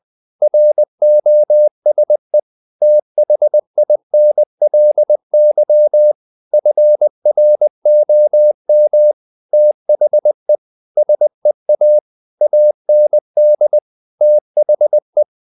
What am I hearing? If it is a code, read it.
Morse code, heard as a keyed tone, about 20 wpm, 600 Hz. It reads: ROSE THINLY FROM THE SEA AND THE
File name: the old man and the sea / the old man and the sea part 1 20wpm